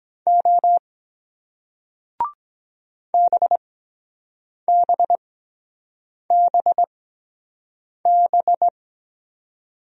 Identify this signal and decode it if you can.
Morse code, heard as a keyed tone, about 23 wpm, spaced as for 7 wpm, 700 Hz. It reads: O BBBB